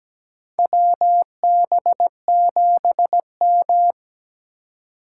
Morse code, heard as a keyed tone, about 17 wpm, 700 Hz. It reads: WB7M
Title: USA Call Signs - 4 Characters - 4x Character Spacing - 17wpm